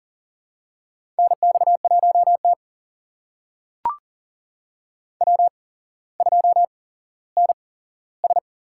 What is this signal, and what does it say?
Morse code, heard as a keyed tone, about 38 wpm, spaced as for 17 wpm, 700 Hz. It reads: NX1T W2NS
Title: USA Call Signs - 4 Characters - 8x Character Spacing - 40wpm